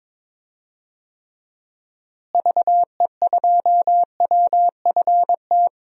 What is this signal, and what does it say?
Morse code, heard as a keyed tone, about 22 wpm, 700 Hz. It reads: VE2WFT
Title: DXCC-100 Call Signs - 6 Characters - Repeated 3x - 22wpm